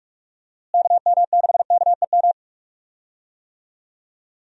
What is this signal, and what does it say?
Morse code, heard as a keyed tone, about 44 wpm, 700 Hz. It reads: KM6KEM
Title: USA Call Signs - 6 Characters - Repeated 2x - 45wpm